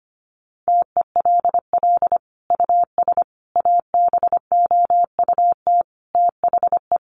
Morse code, heard as a keyed tone, about 25 wpm, 700 Hz. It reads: TELL US ABOUT THE
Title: Sentences from Top 1200 Words 25wpm